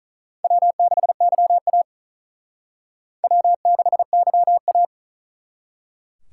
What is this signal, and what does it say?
Morse code, heard as a keyed tone, about 45 wpm, 700 Hz. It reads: W6YA W6YA